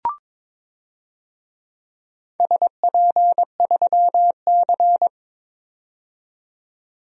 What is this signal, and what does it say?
Morse code, heard as a keyed tone, about 22 wpm, 700 Hz. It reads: SP3C
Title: DXCC-100 Call Signs - 4 Characters - 22wpm